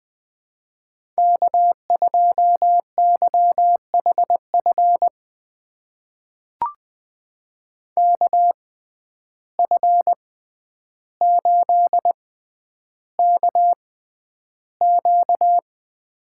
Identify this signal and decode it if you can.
Morse code, heard as a keyed tone, about 20 wpm, spaced as for 10 wpm, 700 Hz. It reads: K2YHF KF8KQ
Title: USA Call Signs - 5 Characters - 6x Character Spacing - 20wpm